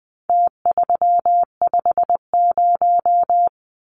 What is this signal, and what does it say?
Morse code, heard as a keyed tone, about 20 wpm, 700 Hz. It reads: T350
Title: Numbers - Contest Style - 4-Digit 20wpm